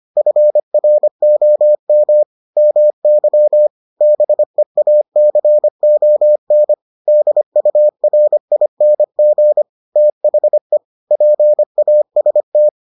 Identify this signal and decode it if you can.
Morse code, heard as a keyed tone, about 25 wpm, 600 Hz. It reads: FROM MY BEACON DURING THE PAST